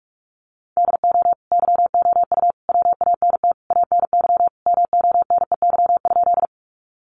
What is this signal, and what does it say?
Morse code, heard as a keyed tone, about 44 wpm, 700 Hz. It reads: DO YOU WANT ANY MONEY?
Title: Sentences from Top 400 Words 45wpm